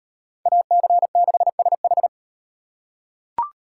Morse code, heard as a keyed tone, about 38 wpm, 700 Hz. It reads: AC6SH